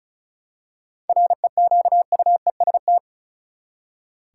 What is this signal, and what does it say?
Morse code, heard as a keyed tone, about 36 wpm, 700 Hz. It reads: REQUEST
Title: Seven Letter Words - 35wpm